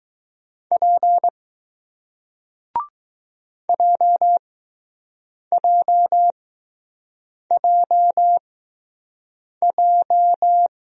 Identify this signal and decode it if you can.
Morse code, heard as a keyed tone, about 20 wpm, spaced as for 6 wpm, 700 Hz. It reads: P JJJJ